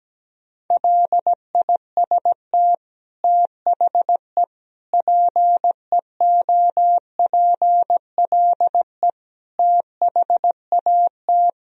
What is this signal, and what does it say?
Morse code, heard as a keyed tone, about 17 wpm, 700 Hz. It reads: LIST THE PEOPLE THAT
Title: Sentences from Top 1000 Words - Review Part 2 - 17wpm